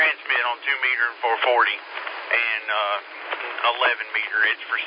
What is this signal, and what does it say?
Emergency Alert System (SAME) tones, heard as harsh buzzing bursts, with speech. It is heard from 0.6 to 4.5 s.